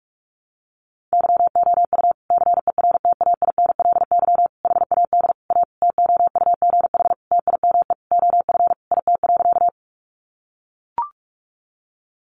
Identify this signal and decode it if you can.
Morse code, heard as a keyed tone, about 44 wpm, 700 Hz. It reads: YOU CERTAINLY HAD A TOUGH TIME OF IT.
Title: Sentences from Top 1200 Words 45wpm